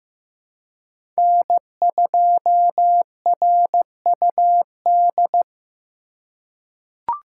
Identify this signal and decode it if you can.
Morse code, heard as a keyed tone, about 15 wpm, 700 Hz. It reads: N2RUD
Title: USA Call Signs - 5 Characters - 2x Character Spacing - 15wpm